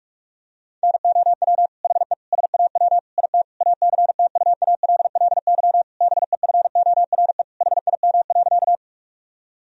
Morse code, heard as a keyed tone, about 41 wpm, 700 Hz. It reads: NOW HE SAW IT ACTUALLY BEFORE HIM.
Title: Sentences from Top 1000 Words - Review Part 4 - 45wpm